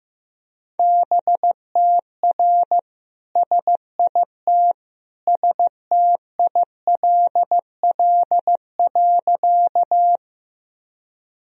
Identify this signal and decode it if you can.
Morse code, heard as a keyed tone, about 16 wpm, 700 Hz. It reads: BTR SIT STILL.
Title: Sentences from Top 500 Words and Abbreviations 15wpm